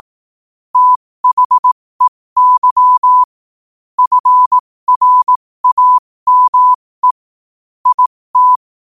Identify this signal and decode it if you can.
Morse code, heard as a keyed tone, about 18 wpm, spaced as for 15 wpm, 1000 Hz. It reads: THEY FRAME IT
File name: qotdfiles / 15wpm